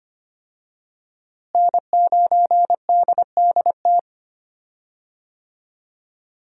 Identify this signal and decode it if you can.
Morse code, heard as a keyed tone, about 25 wpm, 700 Hz. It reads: N9DDT
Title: USA Call Signs - 5 Characters - Repeated 3x - 25wpm